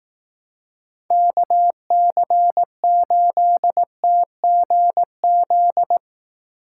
Morse code, heard as a keyed tone, about 18 wpm, 700 Hz. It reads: KC8TGZ
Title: Speed Racing - USA 2x3 Call Signs 15wpm